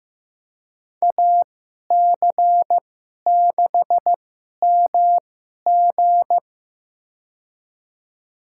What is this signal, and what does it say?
Morse code, heard as a keyed tone, about 16 wpm, spaced as for 11 wpm, 700 Hz. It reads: AC6MG
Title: USA Call Signs - 5 Characters - 2x Character Spacing - 15wpm